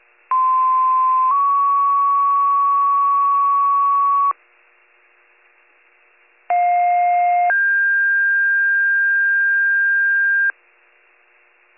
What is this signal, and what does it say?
Two-tone page, heard as a one-second tone, then a three-second tone, about 1040 then 1120 Hz, about 710 then 1600 Hz. It is heard from 0.3 to 10.5 s.